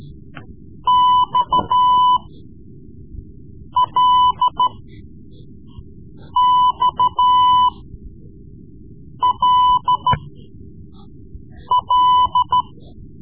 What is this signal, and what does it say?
Morse code, heard as a keyed tone, about 13 wpm, spaced as for 5 wpm, 1000 Hz. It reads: XLXLL
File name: Morse Lessons / Morse Lessons / Tape-3B